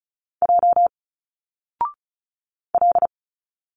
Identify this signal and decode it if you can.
Morse code, heard as a keyed tone, about 36 wpm, 700 Hz. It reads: J L
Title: Single Letters 35wpm